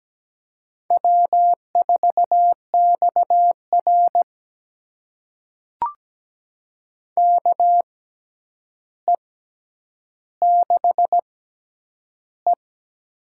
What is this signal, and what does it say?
Morse code, heard as a keyed tone, about 17 wpm, spaced as for 9 wpm, 700 Hz. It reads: W4XR KE6E
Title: USA Call Signs - 4 Characters - 6x Character Spacing - 17wpm